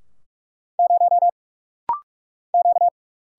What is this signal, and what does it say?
Morse code, heard as a keyed tone, about 46 wpm, 700 Hz. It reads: 0 Q
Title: Single Letter-Number - Rapid-Fire 45wpm